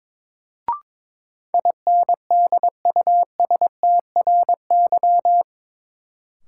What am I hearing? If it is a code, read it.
Morse code, heard as a keyed tone, about 22 wpm, 700 Hz. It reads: INDUSTRY
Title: Sentences from Top 1000 Words - Review Part 4 - New Word Drill - 22wpm